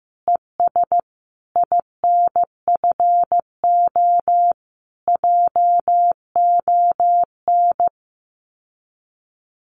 Morse code, heard as a keyed tone, about 15 wpm, 700 Hz. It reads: ES INFO JON
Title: Common QSO Phrases Encore 15wpm